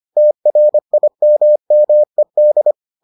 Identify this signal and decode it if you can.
Morse code, heard as a keyed tone, about 25 wpm, 600 Hz. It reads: TRIMMED